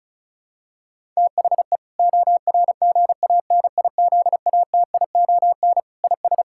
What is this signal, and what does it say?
Morse code, heard as a keyed tone, about 36 wpm, 700 Hz. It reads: THE ORGANIZATION IS
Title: Sentences from Top 1000 Words - Review Part 5 - 35wpm